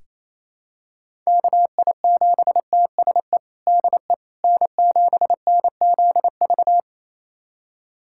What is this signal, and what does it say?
Morse code, heard as a keyed tone, about 27 wpm, 700 Hz. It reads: KI7TSE DE N7NZV
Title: USA Callsign DE Callsign 28wpm